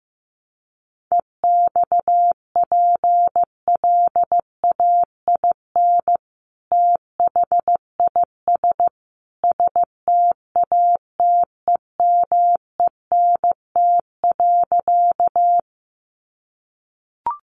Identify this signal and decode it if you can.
Morse code, heard as a keyed tone, about 15 wpm, 700 Hz. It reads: EXPLAIN THIS STATEMENT.
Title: Sentences from Top 1200 Words 15wpm